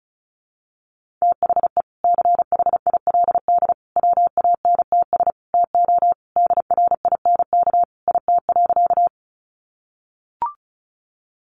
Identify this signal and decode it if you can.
Morse code, heard as a keyed tone, about 34 wpm, 700 Hz. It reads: THE CHILD WANTS TO DRINK IT.
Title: Additional Easy Phrases 35wpm